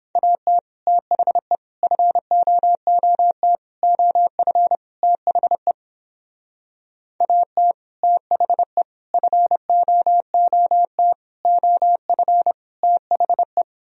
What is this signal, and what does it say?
Morse code, heard as a keyed tone, about 29 wpm, 700 Hz. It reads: AT THE FOOT OF THE AT THE FOOT OF THE